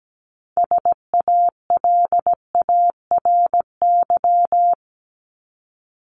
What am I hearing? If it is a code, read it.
Morse code, heard as a keyed tone, about 17 wpm, 700 Hz. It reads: SALARY
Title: Top 1751-2000 Words 17wpm